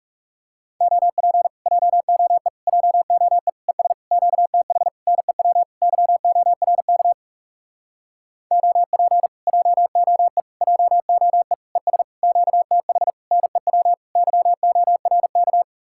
Morse code, heard as a keyed tone, about 37 wpm, 700 Hz. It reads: OP JOE JOE ES QTH NEW YORK OP JOE JOE ES QTH NEW YORK
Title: Speed-Racing Common QSO Phrases Encore - 30wpm